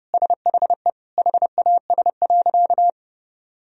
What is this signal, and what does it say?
Morse code, heard as a keyed tone, about 30 wpm, 700 Hz. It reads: SHE HAS.
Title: Sentences from Top 1000 Words - Review Part 2 - 30wpm